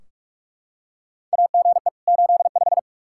Morse code, heard as a keyed tone, about 41 wpm, 700 Hz. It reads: AGE 85
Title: Practice QSO - Exchange 3 - More Info - 45wpm